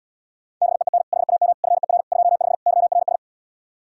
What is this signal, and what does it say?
Morse code, heard as a keyed tone, about 77 wpm, 700 Hz. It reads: BEEN HAM FER 25 YRS